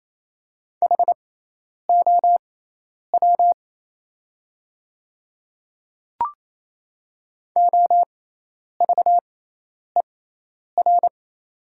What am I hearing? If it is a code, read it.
Morse code, heard as a keyed tone, about 28 wpm, spaced as for 10 wpm, 700 Hz. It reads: HOW OVER
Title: 6x Character Spacing - Top 100 Words - 28wpm